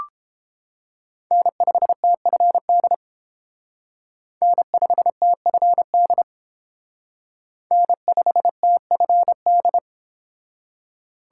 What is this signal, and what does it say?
Morse code, heard as a keyed tone, about 28 wpm, 700 Hz. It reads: N5TFD N5TFD N5TFD